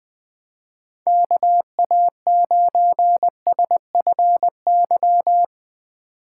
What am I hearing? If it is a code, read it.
Morse code, heard as a keyed tone, about 22 wpm, 700 Hz. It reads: KA9SFY
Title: Speed Racing - USA 2x3 Call Signs 20wpm